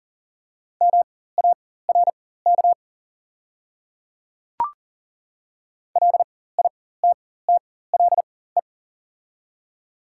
Morse code, heard as a keyed tone, about 40 wpm, spaced as for 19 wpm, 700 Hz. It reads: MARK LITTLE
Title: 4x Character Spacing - Top 300 Words - 40wpm